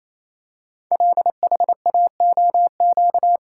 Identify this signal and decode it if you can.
Morse code, heard as a keyed tone, about 27 wpm, 700 Hz. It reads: LHAOQ
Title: Code Groups - 5 Letters - 28wpm